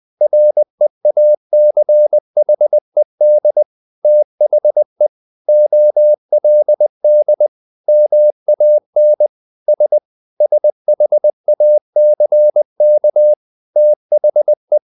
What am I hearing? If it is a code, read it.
Morse code, heard as a keyed tone, about 20 wpm, 600 Hz. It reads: REACHED THE OLD MAN S SHACK THE
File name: the old man and the sea / the old man and the sea part 1 20wpm